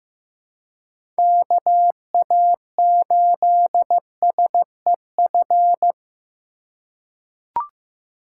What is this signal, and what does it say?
Morse code, heard as a keyed tone, about 15 wpm, 700 Hz. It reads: KA8SEF
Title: USA Call Signs - 2x Character Spacing - 15wpm